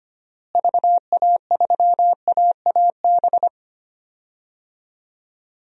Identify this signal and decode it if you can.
Morse code, heard as a keyed tone, about 25 wpm, 700 Hz. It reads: VA3AAB
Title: DXCC-100 Call Signs - 6 Characters - Repeated 3x - 25wpm